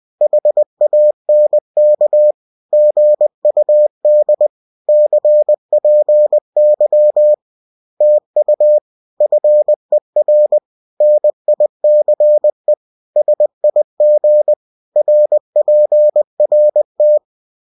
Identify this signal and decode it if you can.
Morse code, heard as a keyed tone, about 20 wpm, 600 Hz. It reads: HANK GUD CPY TU FER NICE SIG RPRT